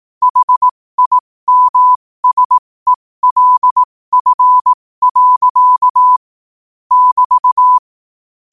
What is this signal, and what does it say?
Morse code, heard as a keyed tone, about 18 wpm, spaced as for 15 wpm, 1000 Hz. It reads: HIMSELF. <BT>
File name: qotdfiles / 15wpm